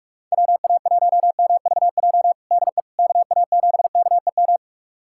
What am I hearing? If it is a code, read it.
Morse code, heard as a keyed tone, about 43 wpm, 700 Hz. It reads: WA1MVJ DE KA7KEM
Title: USA Callsign DE Callsign - 6 Characters - 45wpm